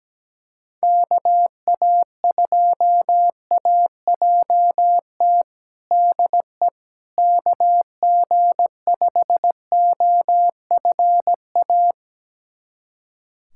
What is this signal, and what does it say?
Morse code, heard as a keyed tone, about 17 wpm, 700 Hz. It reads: KA2AJT DE KG5OFA
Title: USA Callsign DE Callsign - 6 Characters - 17wpm